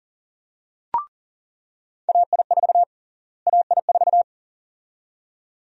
Morse code, heard as a keyed tone, about 40 wpm, 700 Hz. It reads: AI4 AI4
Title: USA Call Sign Prefixes - Repeated 2x - 40wpm